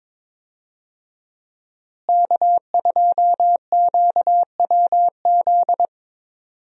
Morse code, heard as a keyed tone, about 22 wpm, 700 Hz. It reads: K2QWZ
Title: USA Call Signs - 5 Characters - Repeated 2x - 22wpm